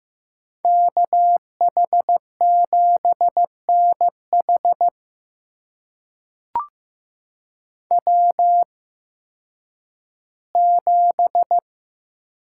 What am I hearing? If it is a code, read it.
Morse code, heard as a keyed tone, about 15 wpm, spaced as for 10 wpm, 700 Hz. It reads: KH7NH W7
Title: USA Call Signs - 5 Characters - 8x Character Spacing - 15wpm